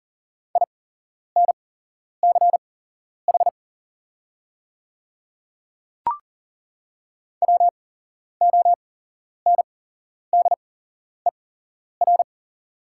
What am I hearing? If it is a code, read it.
Morse code, heard as a keyed tone, about 38 wpm, spaced as for 11 wpm, 690 Hz. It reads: INCH WONDER
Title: 8x Character Spacing - Top 500 Words - 40wpm